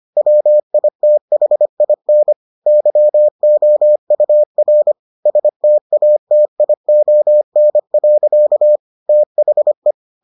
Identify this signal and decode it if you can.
Morse code, heard as a keyed tone, about 25 wpm, 600 Hz. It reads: WITHIN YOUR STATION. THE